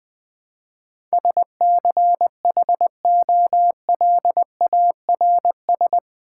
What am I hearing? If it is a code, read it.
Morse code, heard as a keyed tone, about 21 wpm, 700 Hz. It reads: SCHOLARS
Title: Build A Word - 4 Letters At A Time - 11-Letter Words - 20wpm